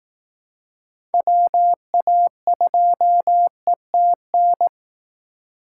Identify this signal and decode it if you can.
Morse code, heard as a keyed tone, about 19 wpm, 700 Hz. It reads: WA2ETN
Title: Speed Racing - USA 2x3 Call Signs 15wpm